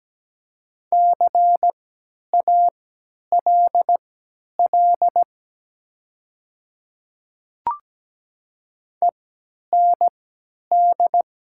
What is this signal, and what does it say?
Morse code, heard as a keyed tone, about 17 wpm, spaced as for 10 wpm, 700 Hz. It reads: CALL END
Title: 3x Character Spacing - Top 200 Words - 17wpm